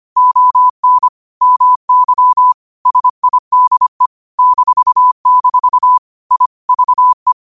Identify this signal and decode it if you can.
Morse code, heard as a keyed tone, about 25 wpm, 1000 Hz. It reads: ON MY SIDE -- I VE
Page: Quotes for Wed, 13 Aug 2025 in Morse Code at 25 words per minute.